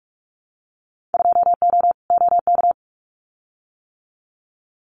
Morse code, heard as a keyed tone, about 44 wpm, 700 Hz. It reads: 2O OK
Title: Winter Field Day Exchanges Encore 45wpm